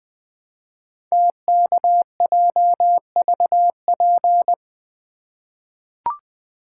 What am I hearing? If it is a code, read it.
Morse code, heard as a keyed tone, about 20 wpm, 700 Hz. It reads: TKJVP
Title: Code Groups - 5 Letters - 20wpm